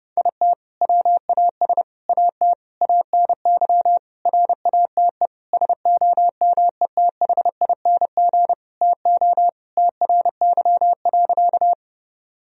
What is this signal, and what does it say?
Morse code, heard as a keyed tone, about 30 wpm, 700 Hz. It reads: IT WAS AT ANY RATE SOMETHING TO TRY.
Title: Sentences from Top 1000 Words - Review Part 4 - 30wpm